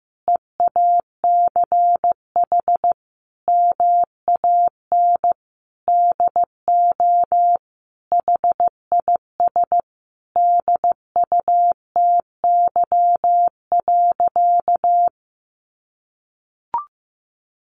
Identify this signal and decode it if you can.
Morse code, heard as a keyed tone, about 15 wpm, 700 Hz. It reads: EACH MAN DO HIS DUTY.
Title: Sentences from Top 1500 Words 15wpm